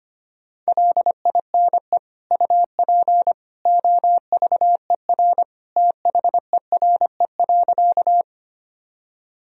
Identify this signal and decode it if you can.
Morse code, heard as a keyed tone, about 25 wpm, 700 Hz. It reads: LINE UP OVER THERE.
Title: Sentences from Top 1000 Words - Review Part 1 - 25wpm